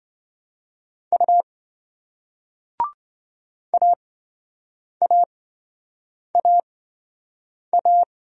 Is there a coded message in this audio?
Yes: Morse code from 1.1 to 8.0 s, heard as a keyed tone, about 26 wpm, spaced as for 7 wpm, 700 Hz. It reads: U AAAA